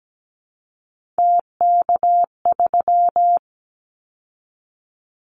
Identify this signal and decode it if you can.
Morse code, heard as a keyed tone, about 17 wpm, 700 Hz. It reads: TK3
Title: Callsign Prefixes for Top 100 DXCC - 17wpm